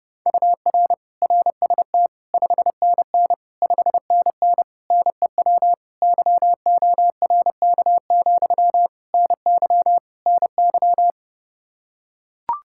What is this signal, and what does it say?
Morse code, heard as a keyed tone, about 29 wpm, 700 Hz. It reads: UR RST 5NN 5NN NEW YORK, NY NY
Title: Practice QSO - Exchange 1 - Beginning - 30wpm